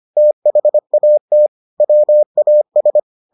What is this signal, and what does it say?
Morse code, heard as a keyed tone, about 25 wpm, 600 Hz. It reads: THAT WAS